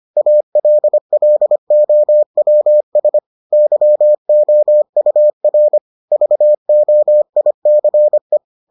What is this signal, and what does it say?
Morse code, heard as a keyed tone, about 25 wpm, 600 Hz. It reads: ALLOWS YOUR VOICE